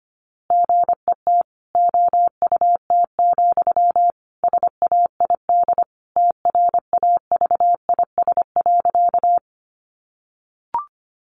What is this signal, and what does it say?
Morse code, heard as a keyed tone, about 25 wpm, 700 Hz. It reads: GET OUT, SAID TRAVIS.
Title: A Sound Of Thunder 25wpm